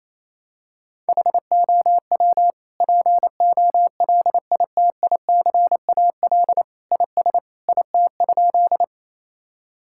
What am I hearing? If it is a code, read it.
Morse code, heard as a keyed tone, about 28 wpm, 700 Hz. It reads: HOW POLITICAL IS IT?
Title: Sentences from Top 1000 Words - Review Part 5 - 28wpm